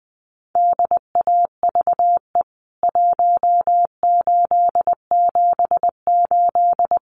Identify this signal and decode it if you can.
Morse code, heard as a keyed tone, about 20 wpm, 700 Hz. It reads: DAVE 1878
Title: CWT Exchange 20wpm